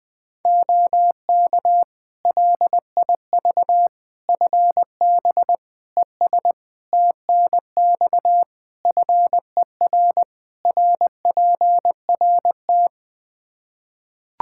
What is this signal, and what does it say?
Morse code, heard as a keyed tone, about 20 wpm, 700 Hz. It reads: OK LIV FB ES TNX FER RPRT
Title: Practice QSO - Exchange 2 - Info - 20wpm